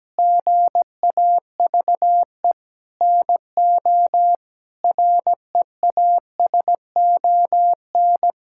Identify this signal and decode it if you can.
Morse code, heard as a keyed tone, about 17 wpm, 700 Hz. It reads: GAVE NO REASON
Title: Sentences from Top 1600 Words 17wpm